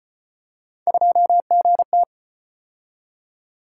Morse code, heard as a keyed tone, about 31 wpm, 700 Hz. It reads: 2GT
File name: Speed-Racing - Three Letter-Number Combinations - 25wpm